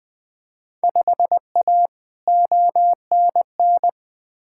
Morse code, heard as a keyed tone, about 20 wpm, 700 Hz. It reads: 5A ONN
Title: Field Day Exchanges Encore - 20wpm